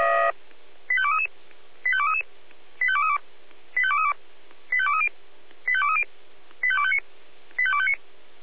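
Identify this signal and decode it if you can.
Five-tone selective call (ZVEI), heard as quick runs of five tones, tones of about 70 ms. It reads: 85233 85233 85232 85232 85230 85230 85259 85259
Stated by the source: ZVEI five-tone selective call received by SDR.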